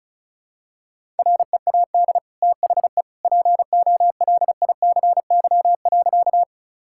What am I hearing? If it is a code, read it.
Morse code, heard as a keyed tone, about 34 wpm, 700 Hz. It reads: READ THE POLICY.
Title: Sentences from Top 1000 Words - Review Part 5 - 35wpm